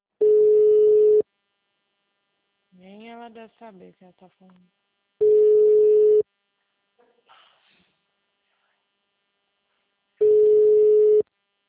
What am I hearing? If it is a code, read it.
Ringback tone (Europe, 425 Hz), with speech.